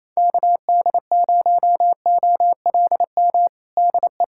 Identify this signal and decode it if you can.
Morse code, heard as a keyed tone, about 28 wpm, 700 Hz. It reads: KD0OLM DE
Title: USA Callsign DE Callsign - 6 Characters - 28wpm